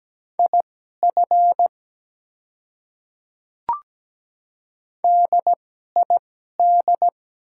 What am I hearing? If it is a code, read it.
Morse code, heard as a keyed tone, about 17 wpm, spaced as for 12 wpm, 700 Hz. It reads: IF DID
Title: 2x Character Spacing - Top 100 Words - 17wpm